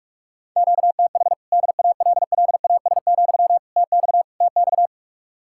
Morse code, heard as a keyed tone, about 56 wpm, 700 Hz. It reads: QTH DALLAS, TX TX
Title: Speed-Racing Common QSO Phrases - 45wpm